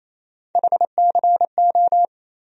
Morse code, heard as a keyed tone, about 28 wpm, 700 Hz. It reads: HCO
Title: USA Call Sign Postfixes - 3 Letters 28wpm